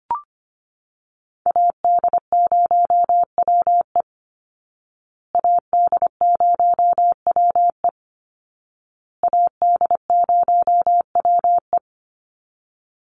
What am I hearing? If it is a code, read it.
Morse code, heard as a keyed tone, about 25 wpm, 700 Hz. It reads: AD0WE AD0WE AD0WE
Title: Call Signs Repeated 2x - 25wpm